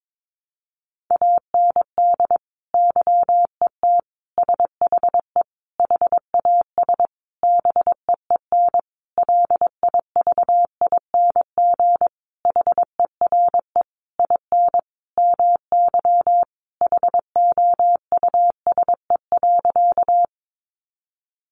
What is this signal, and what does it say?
Morse code, heard as a keyed tone, about 22 wpm, 700 Hz. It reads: AND YET SHE HAS BEEN LIVING HERE IN MY HOUSE.
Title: Sentences from Top 1100 Words 22wpm